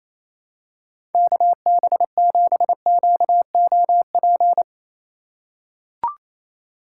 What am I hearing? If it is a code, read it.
Morse code, heard as a keyed tone, about 28 wpm, 700 Hz. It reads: KB7QOP
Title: K1USN SST QSO Practice - 28wpm